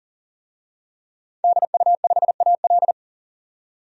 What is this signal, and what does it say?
Morse code, heard as a keyed tone, about 40 wpm, 700 Hz. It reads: DU5AL
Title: DXCC-100 Call Signs - 5 Characters - 40wpm